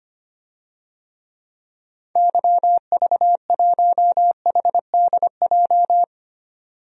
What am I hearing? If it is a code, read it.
Morse code, heard as a keyed tone, about 25 wpm, 700 Hz. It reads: YV1HDJ
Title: DXCC-100 Call Signs - 6 Characters - 25wpm